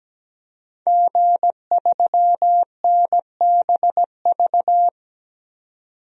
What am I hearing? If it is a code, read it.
Morse code, heard as a keyed tone, about 17 wpm, 700 Hz. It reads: G3NBV
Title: DXCC-100 Call Signs - 5 Characters - 17wpm